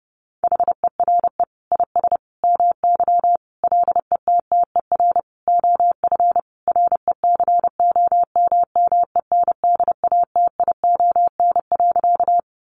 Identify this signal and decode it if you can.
Morse code, heard as a keyed tone, about 30 wpm, 700 Hz. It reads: HERE IS MY LETTER OF RECOMMENDATION.
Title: Sentences from Top 1400 Words 30wpm